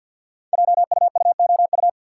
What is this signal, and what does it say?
Morse code, heard as a keyed tone, about 48 wpm, 690 Hz. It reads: JUUOU
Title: Code Groups - 5 Letters - 50wpm